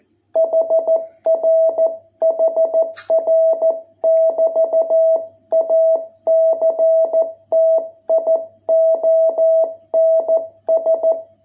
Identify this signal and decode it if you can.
Morse code, heard as a keyed tone, about 14 wpm, 660 Hz. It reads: HR HR <BT> ACTIONS